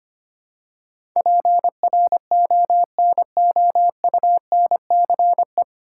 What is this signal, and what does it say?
Morse code, heard as a keyed tone, about 25 wpm, 700 Hz. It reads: PRONOUNCE
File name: Nine Letter Words - 25wpm